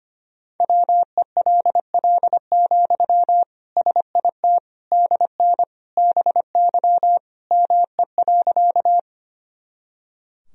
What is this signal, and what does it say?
Morse code, heard as a keyed tone, about 25 wpm, 700 Hz. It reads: WELL, SIT DN BY ME.
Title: Sentences from Top 500 Words and Abbreviations 25wpm